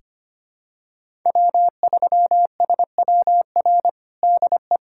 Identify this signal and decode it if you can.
Morse code, heard as a keyed tone, about 25 wpm, 700 Hz. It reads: W3SWR DE
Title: USA Callsign DE Callsign - 5 Characters - 25wpm